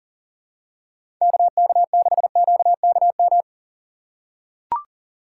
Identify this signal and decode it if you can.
Morse code, heard as a keyed tone, about 38 wpm, 700 Hz. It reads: KK6QKM
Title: USA Call Signs - 6 Characters - 6x Character Spacing - 40wpm